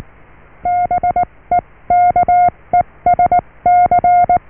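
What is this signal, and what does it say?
Morse code, heard as a keyed tone, about 19 wpm, spaced as for 16 wpm, 710 Hz. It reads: BEKESC
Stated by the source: Morse QSO on 14 MHz with a German station, received at RST 579.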